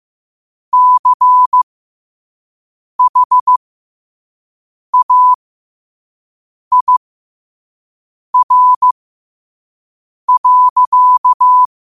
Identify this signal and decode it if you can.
Morse code, heard as a keyed tone, about 15 wpm, spaced as for 5 wpm, 1000 Hz. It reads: CHAIR.
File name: 5wpm